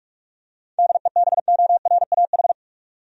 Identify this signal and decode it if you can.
Morse code, heard as a keyed tone, about 46 wpm, 700 Hz. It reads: DEBORAH
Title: USA Common Names - 45wpm